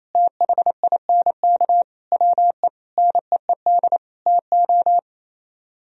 Morse code, heard as a keyed tone, about 31 wpm, 700 Hz. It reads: THINK WE NEED TO
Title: Speed-Racing - Sets of 5 Words 28wpm